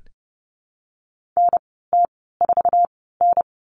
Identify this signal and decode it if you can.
Morse code, heard as a keyed tone, about 30 wpm, spaced as for 17 wpm, 700 Hz. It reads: NT4N
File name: Lesson 15g - Number 5 - Call Signs 30wpm